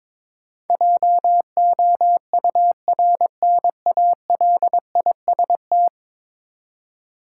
Morse code, heard as a keyed tone, about 22 wpm, 700 Hz. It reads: JOURNALIST
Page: morse-code-ninja-practice-22wpm